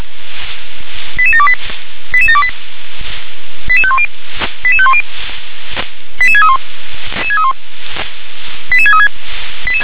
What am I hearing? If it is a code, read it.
Five-tone selective call (ZVEI), heard as quick runs of five tones, tones of about 70 ms. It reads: 88728 88728 88510 88510 88631 88636
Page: SDR/ZVEI